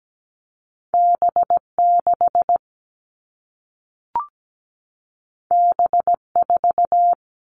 Morse code, heard as a keyed tone, about 17 wpm, 700 Hz. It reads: B6 B4
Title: All Cut Numbers - 2-Digit 17wpm